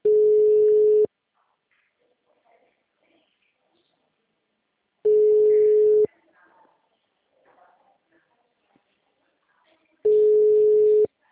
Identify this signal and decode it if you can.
Ringback tone (Europe, 425 Hz), with speech.